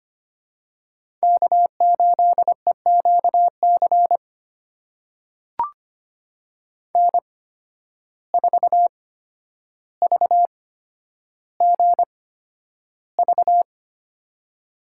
Morse code, heard as a keyed tone, about 25 wpm, spaced as for 11 wpm, 700 Hz. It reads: K8EQC N4VGV
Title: USA Call Signs - 5 Characters - 8x Character Spacing - 25wpm